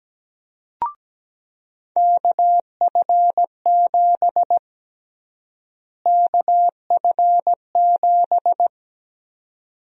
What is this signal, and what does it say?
Morse code, heard as a keyed tone, about 17 wpm, 700 Hz. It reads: KF7 KF7
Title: USA Call Sign Prefixes - Repeated 2x - 17wpm